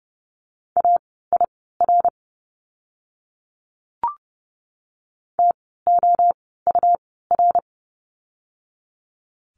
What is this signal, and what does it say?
Morse code, heard as a keyed tone, about 30 wpm, spaced as for 17 wpm, 700 Hz. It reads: AIR TOUR
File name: Lesson 17f - Letter u - Words 30wpm